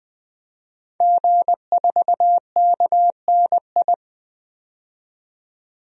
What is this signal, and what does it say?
Morse code, heard as a keyed tone, about 21 wpm, 700 Hz. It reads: G4KNI
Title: DXCC-100 Call Signs - 5 Characters - 20wpm